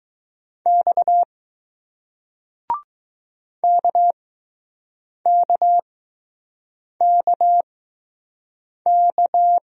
Morse code, heard as a keyed tone, about 20 wpm, spaced as for 6 wpm, 700 Hz. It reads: X KKKK